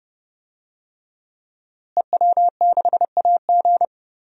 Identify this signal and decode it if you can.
Morse code, heard as a keyed tone, about 30 wpm, 700 Hz. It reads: EW6AG
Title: DXCC-100 Call Signs - 5 Characters - Repeated 3x - 30wpm